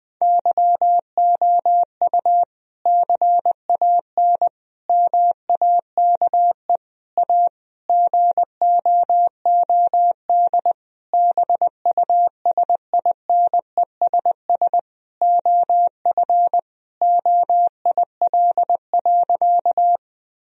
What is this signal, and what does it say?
Morse code, heard as a keyed tone, about 20 wpm, 700 Hz. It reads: YOU CAN MAKE A GOOD BUSINESS OF OIL.
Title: Sentences from Top 1000 Words - Review Part 4 - 20wpm